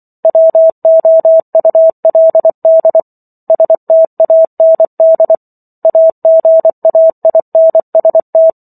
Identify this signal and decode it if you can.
Morse code, heard as a keyed tone, about 24 wpm, 650 Hz. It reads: WOULD STAND AGAINST